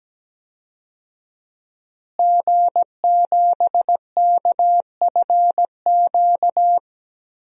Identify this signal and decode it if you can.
Morse code, heard as a keyed tone, about 17 wpm, 700 Hz. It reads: G7KFQ
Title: DXCC-100 Call Signs - 5 Characters - Repeated 2x - 17wpm